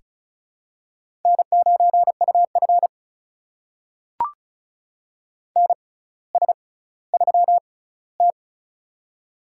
Morse code, heard as a keyed tone, about 36 wpm, spaced as for 18 wpm, 700 Hz. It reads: N9UF NS3T
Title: USA Call Signs - 4 Characters - 6x Character Spacing - 35wpm